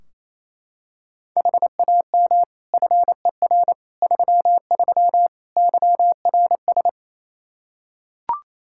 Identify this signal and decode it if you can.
Morse code, heard as a keyed tone, about 27 wpm, 700 Hz. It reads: HAM FER 33 YRS
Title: Practice QSO - Exchange 3 - More Info - 28wpm